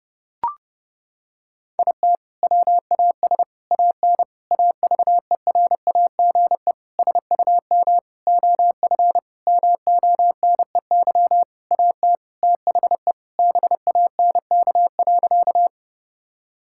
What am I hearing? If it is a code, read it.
Morse code, heard as a keyed tone, about 30 wpm, 700 Hz. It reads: IT WAS AN AVERAGE SUM OF MONEY AT THE BANK.
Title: Sentences from Top 1000 Words - Review Part 6 - 30wpm